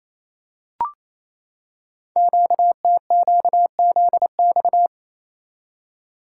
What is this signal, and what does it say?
Morse code, heard as a keyed tone, about 29 wpm, 700 Hz. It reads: QTQZX
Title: Code Groups - 5 Letters - 28wpm